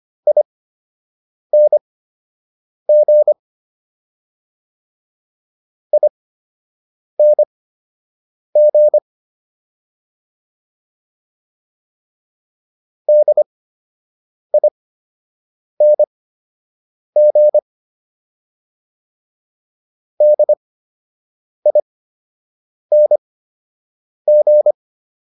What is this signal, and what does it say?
Morse code, heard as a keyed tone, about 25 wpm, spaced as for 7 wpm, 600 Hz. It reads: ING ING DING DING